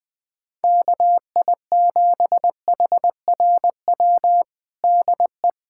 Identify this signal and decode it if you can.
Morse code, heard as a keyed tone, about 20 wpm, 700 Hz. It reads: KI7HRW DE
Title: USA Callsign DE Callsign - 6 Characters - 20wpm